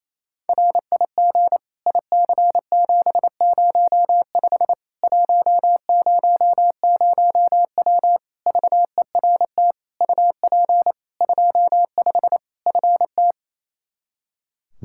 Morse code, heard as a keyed tone, about 28 wpm, 700 Hz. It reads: RIG IC705 100W VERT UP 25 FT
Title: K1USN SST QSO Practice 28wpm